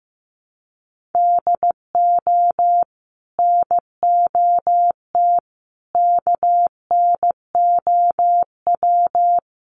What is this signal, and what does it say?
Morse code, heard as a keyed tone, about 15 wpm, 700 Hz. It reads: DO NOT KNOW
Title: Sets of 5 Words Encore2 15wpm